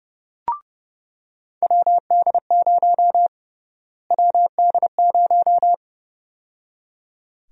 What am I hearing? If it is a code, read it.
Morse code, heard as a keyed tone, about 30 wpm, 700 Hz. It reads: WD0 WD0
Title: USA Call Sign Prefixes - Repeated 2x - 30wpm